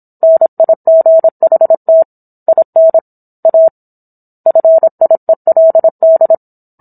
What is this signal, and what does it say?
Morse code, heard as a keyed tone, about 27 wpm, 650 Hz. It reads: NIGHT IN A FIELD